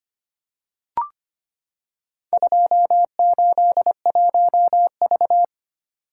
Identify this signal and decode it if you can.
Morse code, heard as a keyed tone, about 25 wpm, 700 Hz. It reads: 281V
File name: practice / 25wpm / Four Letter-Number Combinations - 25wpm